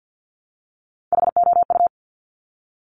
Morse code, heard as a keyed tone, about 50 wpm, 700 Hz. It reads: HOU
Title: Top Three Letter Combinations in Words 50wpm